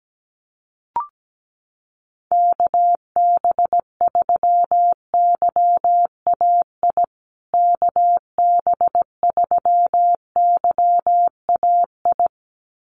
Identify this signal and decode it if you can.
Morse code, heard as a keyed tone, about 17 wpm, 700 Hz. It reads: KB3YAI KB3YAI
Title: USA Call Signs Repeated 2x 17wpm